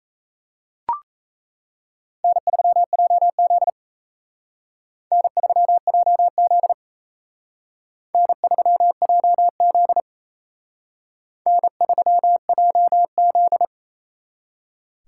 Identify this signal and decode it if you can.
Morse code, heard as a keyed tone, about 34 wpm, 700 Hz. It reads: N3JZ N3JZ N3JZ N3JZ
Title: Speed Racing - USA Call Signs 28wpm